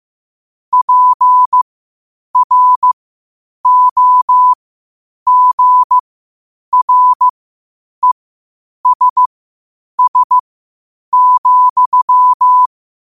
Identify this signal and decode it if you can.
Morse code, heard as a keyed tone, about 15 wpm, spaced as for 8 wpm, 1000 Hz. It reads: PROGRESS,
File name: qotdfiles / 8wpm